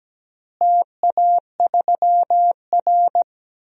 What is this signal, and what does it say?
Morse code, heard as a keyed tone, about 17 wpm, 700 Hz. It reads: TA3R
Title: DXCC-100 Call Signs - 4 Characters - Repeated 2x - 17wpm